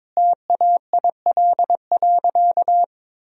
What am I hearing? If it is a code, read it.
Morse code, heard as a keyed tone, about 22 wpm, 700 Hz. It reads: TAIL.
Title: Sentences from Top 1000 Words - Review Part 3 - 22wpm